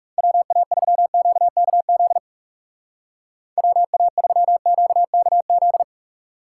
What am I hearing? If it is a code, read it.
Morse code, heard as a keyed tone, about 39 wpm, 700 Hz. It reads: WA3QKZ WA3QKZ